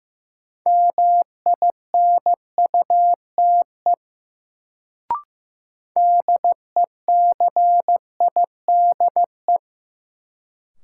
Morse code, heard as a keyed tone, about 15 wpm, 700 Hz. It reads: MINUTE DECIDE
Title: Sentences from Top 500 Words - New Word Drill - 15wpm